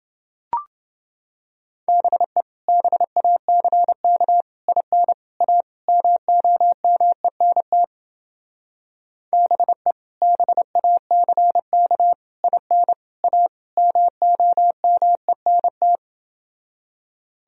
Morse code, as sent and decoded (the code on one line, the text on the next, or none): -... / -... .- -.-. -.- / .. -. / .- / -- --- -- . -. - / -... / -... .- -.-. -.- / .. -. / .- / -- --- -- . -. -
BE BACK IN A MOMENT BE BACK IN A MOMENT